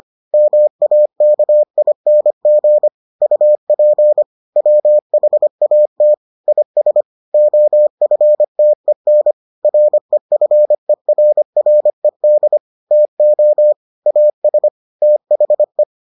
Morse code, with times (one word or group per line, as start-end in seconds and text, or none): 0.3-2.9 MAKING
3.2-4.2 UP
4.6-6.1 WHAT
6.5-7.0 IS
7.3-9.3 OFTEN
9.6-12.6 REFERRED
12.9-13.7 TO
14.1-14.7 AS
15.0-15.8 THE